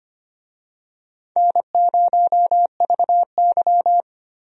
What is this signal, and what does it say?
Morse code, heard as a keyed tone, about 25 wpm, 700 Hz. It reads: N0VY
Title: USA Call Signs - 4 Characters - Repeated 2x - 25wpm